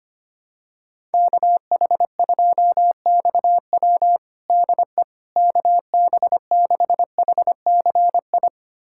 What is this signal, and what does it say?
Morse code, heard as a keyed tone, about 25 wpm, 700 Hz. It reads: KH2XW DE KB6HCI
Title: USA Callsign DE Callsign 25wpm